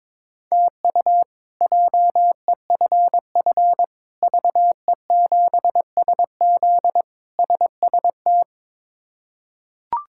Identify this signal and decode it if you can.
Morse code, heard as a keyed tone, about 22 wpm, 700 Hz. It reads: TU JEFF VE7SZ SST
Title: K1USN SST QSO Practice - 22wpm